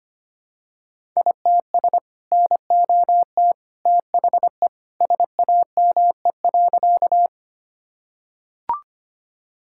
Morse code, heard as a keyed tone, about 25 wpm, 700 Hz. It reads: ITS NOT THE SAME.
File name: practice / 25wpm / Sentences with 2-Word Phrases - 25wpm